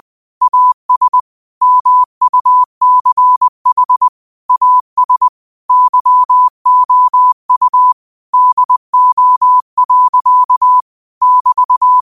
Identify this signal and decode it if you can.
Morse code, heard as a keyed tone, about 20 wpm, 1000 Hz. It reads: AS MUCH AS YOU DO. <BT>